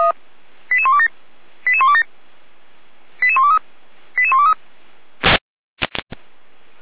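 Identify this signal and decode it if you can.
Five-tone selective call (ZVEI), heard as quick runs of five tones, tones of about 70 ms. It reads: 88127 88127 88123 88123
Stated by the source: Five-tone ZVEI selective call received by SDR, recorded by feuerwehr-orlamuende fire brigade.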